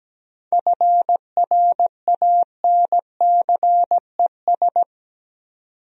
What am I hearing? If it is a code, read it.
Morse code, heard as a keyed tone, about 17 wpm, 700 Hz. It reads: FRANCES
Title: USA Common Names - 17wpm